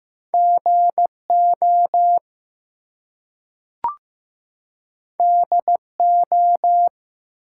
Morse code, heard as a keyed tone, about 15 wpm, 700 Hz. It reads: GO DO
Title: Two Letter Words - 15wpm